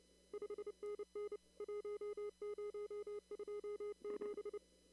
Morse code, heard as a keyed tone, about 28 wpm, 410 Hz. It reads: 5NN 1027
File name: CW / kt5j wpx cw-op-k5pi